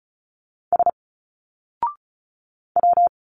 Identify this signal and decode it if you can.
Morse code, heard as a keyed tone, about 35 wpm, 700 Hz. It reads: S W
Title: Single Letters 35wpm